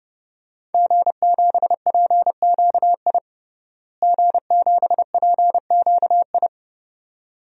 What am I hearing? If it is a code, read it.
Morse code, heard as a keyed tone, about 32 wpm, 700 Hz. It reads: G7PQI G7PQI
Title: DXCC-100 Call Signs - 5 Characters - Repeated 2x - 30wpm